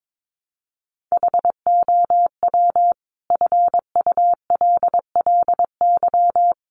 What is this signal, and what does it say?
Morse code, heard as a keyed tone, about 22 wpm, 700 Hz. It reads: HOW FULLY
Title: Sentences from Top 1300 Words 22wpm